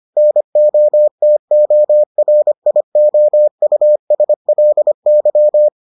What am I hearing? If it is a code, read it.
Morse code, heard as a keyed tone, about 25 wpm, 600 Hz. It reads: NOTORIOUSLY